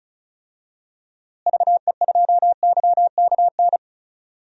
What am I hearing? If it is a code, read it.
Morse code, heard as a keyed tone, about 36 wpm, 700 Hz. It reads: VE2YKN
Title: DXCC-100 Call Signs - 6 Characters - Repeated 3x - 35wpm